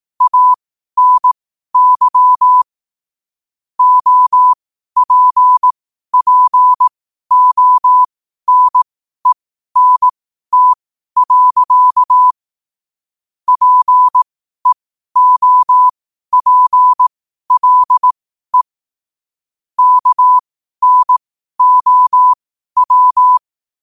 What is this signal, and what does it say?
Morse code, heard as a keyed tone, about 18 wpm, spaced as for 12 wpm, 1000 Hz. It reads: ANY OPPONENT. PEOPLE KNOW
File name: qotdfiles / 12wpm